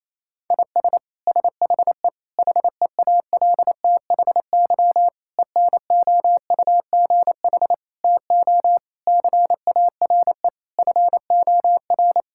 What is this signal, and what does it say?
Morse code, heard as a keyed tone, about 28 wpm, 700 Hz. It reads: IS SHE HEALTHY ENOUGH TO CARE FOR
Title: Sentences from Top 1000 Words - Review Part 5 - 28wpm